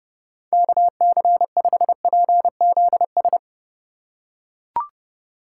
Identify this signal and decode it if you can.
Morse code, heard as a keyed tone, about 30 wpm, 700 Hz. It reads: KC5PZS